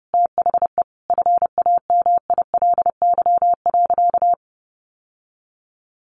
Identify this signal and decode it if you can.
Morse code, heard as a keyed tone, about 30 wpm, 700 Hz. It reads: THE FAMILY.
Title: Sentences from Top 1100 Words 30wpm